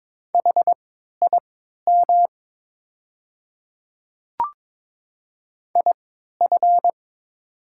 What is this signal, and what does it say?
Morse code, heard as a keyed tone, about 23 wpm, spaced as for 13 wpm, 700 Hz. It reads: HIM IF
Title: 3x Character Spacing - Top 100 Words - 22wpm